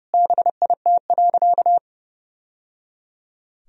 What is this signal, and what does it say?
Morse code, heard as a keyed tone, about 30 wpm, 700 Hz. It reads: BIT.
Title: Sentences from Top 1000 Words - Review Part 1 - 30wpm